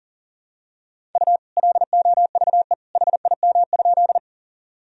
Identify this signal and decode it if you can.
Morse code, heard as a keyed tone, about 40 wpm, 700 Hz. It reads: U LOVE HIM?
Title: Sentences from Top 400 Words and Abbreviations 40wpm